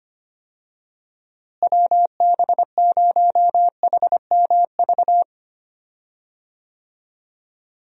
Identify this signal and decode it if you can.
Morse code, heard as a keyed tone, about 25 wpm, 700 Hz. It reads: WB0HMV
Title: USA Call Signs - 6 Characters - Repeated 3x - 25wpm